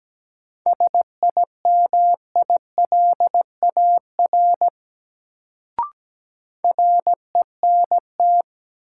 Morse code, heard as a keyed tone, about 17 wpm, 700 Hz. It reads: SIMILAR RENT
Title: Sentences from Top 1000 Words - Review Part 5 - New Word Drill - 17wpm